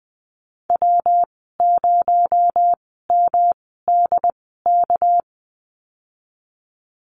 Morse code, heard as a keyed tone, about 20 wpm, spaced as for 14 wpm, 700 Hz. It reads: W0MDK
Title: 231 Lesson 37g - Letter x - Call Signs 20wpm